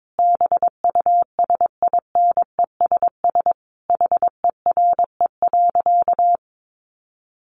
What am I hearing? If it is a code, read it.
Morse code, heard as a keyed tone, about 22 wpm, 700 Hz. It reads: BUSINESS HERE.
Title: Sentences from Top 600 Words 22wpm